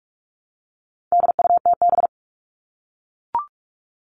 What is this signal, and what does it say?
Morse code, heard as a keyed tone, about 44 wpm, 700 Hz. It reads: DUTB
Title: All Cut Numbers - 4-Digit 45wpm